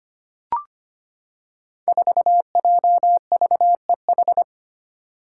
Morse code, heard as a keyed tone, about 25 wpm, 700 Hz. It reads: 4JVEH
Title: Five Challenging Letter-Number Combinations 25wpm